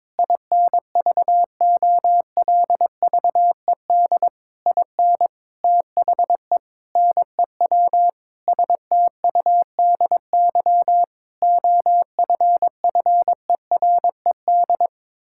Morse code, heard as a keyed tone, about 22 wpm, 700 Hz. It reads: INVOLVED IN THE NEW STUDY OFFERED
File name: practice / 22wpm / Sentences with 8-Word Phrases - 22wpm